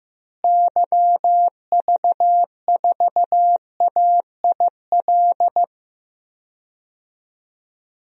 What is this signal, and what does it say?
Morse code, heard as a keyed tone, about 16 wpm, 700 Hz. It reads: YV4AIL
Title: DXCC-100 Call Signs - 6 Characters - Repeated 2x - 15wpm